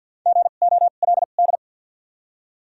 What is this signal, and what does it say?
Morse code, heard as a keyed tone, about 53 wpm, spaced as for 37 wpm, 700 Hz. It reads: GOLD
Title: 2x Character Spacing - Top 500 Words - 50wpm